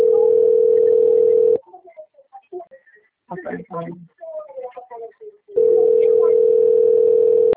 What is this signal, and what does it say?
Ringback tone (North America), 440 + 480 Hz, with speech.
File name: out- - - - - .46540